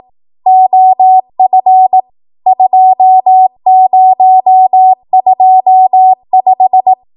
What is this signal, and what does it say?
Morse code, heard as a keyed tone, about 18 wpm, 750 Hz. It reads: OF 2025
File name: Morse / 18WPM